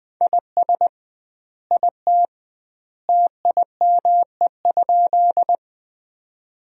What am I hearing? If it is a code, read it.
Morse code, heard as a keyed tone, about 21 wpm, 700 Hz. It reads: IS IT TIME?
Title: 2x Word-Spacing - Sentences from Top 100 Words - 20wpm